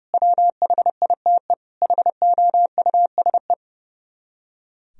Morse code, heard as a keyed tone, about 36 wpm, 700 Hz. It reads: WHITE HOUSE
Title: Speed-Racing - Sets of 4 Words 30wpm